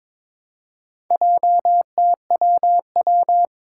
Morse code, heard as a keyed tone, about 22 wpm, 700 Hz. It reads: JTWW
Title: Four Challenging Letter-Number Combinations 22wpm